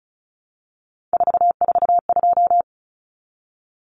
Morse code, heard as a keyed tone, about 34 wpm, 700 Hz. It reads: 442
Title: Numbers - 3-Digit 35wpm